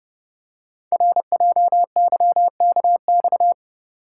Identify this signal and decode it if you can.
Morse code, heard as a keyed tone, about 30 wpm, 700 Hz. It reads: RJYKX
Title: Code Groups - 5 Letters - 30wpm